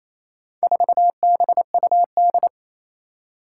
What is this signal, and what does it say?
Morse code, heard as a keyed tone, about 28 wpm, 700 Hz. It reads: 4BUD
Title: Four Challenging Letter-Number Combinations 28wpm